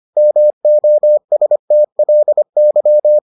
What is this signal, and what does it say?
Morse code, heard as a keyed tone, about 25 wpm, 600 Hz. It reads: MOSTLY